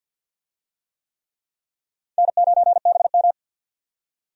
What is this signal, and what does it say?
Morse code, heard as a keyed tone, about 50 wpm, 700 Hz. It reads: N9BM
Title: USA Call Signs - 4 Characters - 50wpm